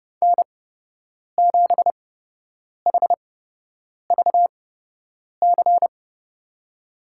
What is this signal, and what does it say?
Morse code, heard as a keyed tone, about 30 wpm, spaced as for 8 wpm, 700 Hz. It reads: N7HVC
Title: USA Call Signs - 5 Characters - 8x Character Spacing - 30wpm